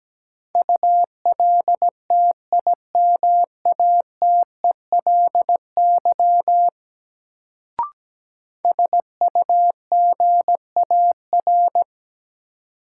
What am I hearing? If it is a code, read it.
Morse code, heard as a keyed tone, about 17 wpm, 700 Hz. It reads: ULTIMATELY SUGAR
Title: Sentences from Top 1200 Words - New Word Drill - 17wpm